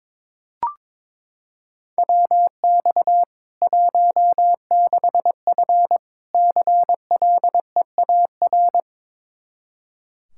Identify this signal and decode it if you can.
Morse code, heard as a keyed tone, about 22 wpm, 700 Hz. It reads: WX 16F CLEAR
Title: Practice QSO - Exchange 2 - Info - 22wpm